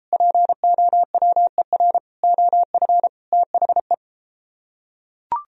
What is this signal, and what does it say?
Morse code, heard as a keyed tone, about 34 wpm, 700 Hz. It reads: POWER OF THE